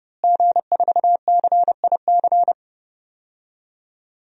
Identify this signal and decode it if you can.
Morse code, heard as a keyed tone, about 30 wpm, 700 Hz. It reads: G4CIC
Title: DXCC-100 Call Signs - 5 Characters - Repeated 3x - 30wpm